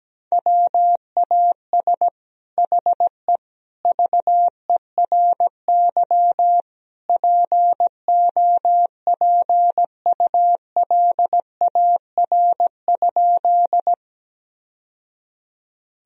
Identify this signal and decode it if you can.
Morse code, heard as a keyed tone, about 17 wpm, 700 Hz. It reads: WAS HE VERY POPULAR?
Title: Sentences from Top 1000 Words - Review Part 3 - 17wpm